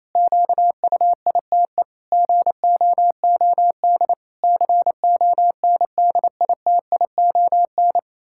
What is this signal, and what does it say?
Morse code, heard as a keyed tone, about 28 wpm, 700 Hz. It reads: QUITE GOOD CONDITION
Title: Pangram Challenge v2 - 28wpm